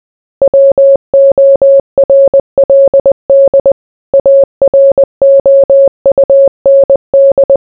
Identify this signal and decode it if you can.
Morse code, heard as a keyed tone, about 20 wpm, 550 Hz. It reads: WORLD AROUND